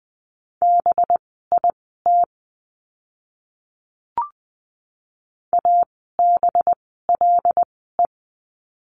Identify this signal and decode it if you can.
Morse code, heard as a keyed tone, about 20 wpm, spaced as for 14 wpm, 700 Hz. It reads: BIT ABLE
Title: 185 Lesson 30f - Letter b - Words 20wpm